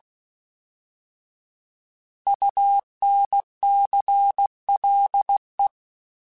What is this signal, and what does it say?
Morse code, heard as a keyed tone, about 16 wpm, 800 Hz. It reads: UNCLE